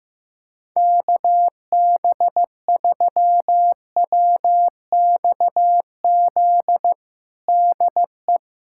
Morse code, heard as a keyed tone, about 15 wpm, 700 Hz. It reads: KB3WXZ DE
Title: USA Callsign DE Callsign - 6 Characters - 15wpm